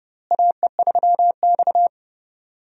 Morse code, heard as a keyed tone, about 30 wpm, 700 Hz. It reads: AE3X
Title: USA Call Signs - 4 Characters - Repeated 2x - 30wpm